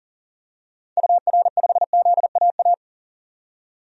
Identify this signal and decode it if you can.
Morse code, heard as a keyed tone, about 40 wpm, 700 Hz. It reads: UR5ZAA
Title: DXCC-100 Call Signs - 6 Characters - 40wpm